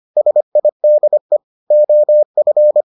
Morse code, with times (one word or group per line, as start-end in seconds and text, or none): 0.2-1.4 SIDE
1.7-2.8 OF